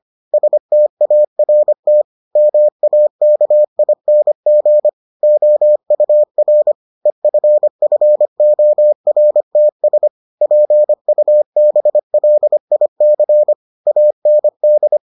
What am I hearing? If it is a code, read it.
Morse code, heard as a keyed tone, about 25 wpm, 600 Hz. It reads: START MAKING OUR EFFORTS PUBLIC AND